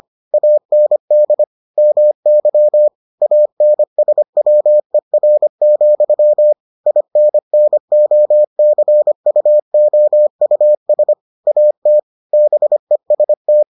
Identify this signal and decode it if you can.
Morse code, heard as a keyed tone, about 25 wpm, 600 Hz. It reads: AND MY ANSWER, INNOCUOUS AT BEST